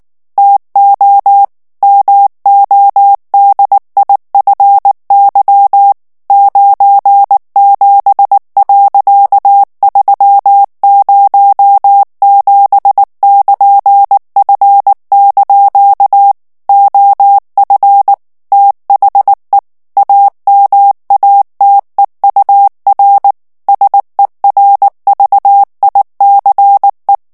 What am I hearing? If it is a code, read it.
Morse code, heard as a keyed tone, about 19 wpm, 800 Hz. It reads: TO MODIFY 97.307<KN>F) OF THE AMATEUR SERVICE
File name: ANGLAIS texte clair 3 vitesse 18 mots minute